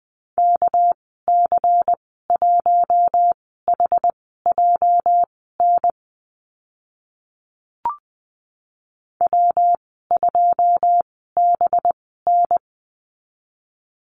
Morse code, heard as a keyed tone, about 20 wpm, spaced as for 14 wpm, 700 Hz. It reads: KC1HJN W2BN
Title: 213 Lesson 34g - Number 8 - Call Signs 20wpm